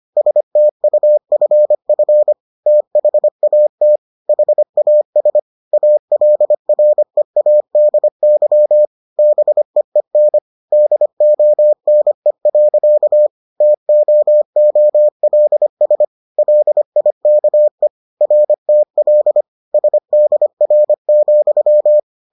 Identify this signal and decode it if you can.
Morse code, heard as a keyed tone, about 25 wpm, 600 Hz. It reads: STUFF THAT HAS ALREADY BEEN DONE. TOOLS LIKE RTL SDR,